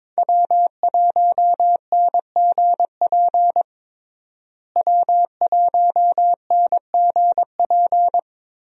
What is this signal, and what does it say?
Morse code, heard as a keyed tone, about 22 wpm, 700 Hz. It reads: W1NGP W1NGP
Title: USA Call Signs - 5 Characters - Repeated 3x - 22wpm